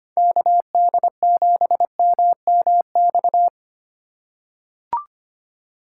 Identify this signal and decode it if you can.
Morse code, heard as a keyed tone, about 25 wpm, 700 Hz. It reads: KD7MMX
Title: USA Call Signs - 6 Characters - 3x Character Spacing - 25wpm